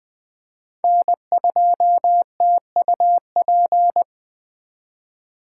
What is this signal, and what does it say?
Morse code, heard as a keyed tone, about 20 wpm, 700 Hz. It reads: N2TUP
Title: Speed Racing - USA 1x3 Call Signs 17wpm